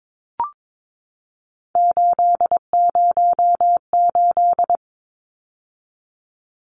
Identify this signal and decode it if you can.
Morse code, heard as a keyed tone, about 22 wpm, 700 Hz. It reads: 808
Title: Numbers - 3-Digit 22wpm